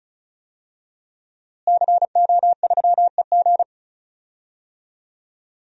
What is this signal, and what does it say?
Morse code, heard as a keyed tone, about 36 wpm, 700 Hz. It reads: CO3EG
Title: DXCC-100 Call Signs - 5 Characters - Repeated 2x - 35wpm